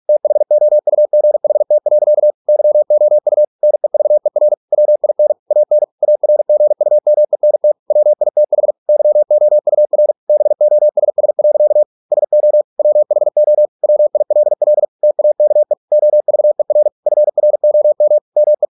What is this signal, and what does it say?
Morse code, heard as a keyed tone, about 46 wpm, 600 Hz. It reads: THOUGHT. YOU NEVER WIN AN ARGUMENT WITH YOUR BOSS. SO WHO WILL TAKE OVER FROM ME